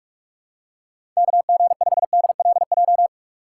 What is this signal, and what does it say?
Morse code, heard as a keyed tone, about 44 wpm, 700 Hz. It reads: KG5DLJ
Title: USA Call Signs - 6x Character Spacing - 45wpm